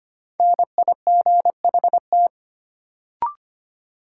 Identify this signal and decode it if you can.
Morse code, heard as a keyed tone, about 25 wpm, 700 Hz. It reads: NIGHT
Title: Sets of 4 Words - New Word Drill - 25wpm